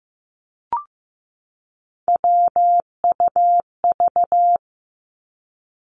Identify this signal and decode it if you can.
Morse code, heard as a keyed tone, about 15 wpm, 700 Hz. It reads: WUV
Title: Three Challenging Letter-Number Combinations 15wpm